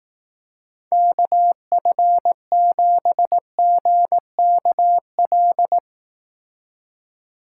Morse code, heard as a keyed tone, about 18 wpm, 700 Hz. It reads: KF7GKL
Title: Speed Racing - USA 2x3 Call Signs 15wpm